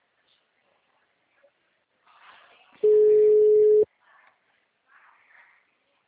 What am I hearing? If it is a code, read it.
Ringback tone (Europe, 425 Hz).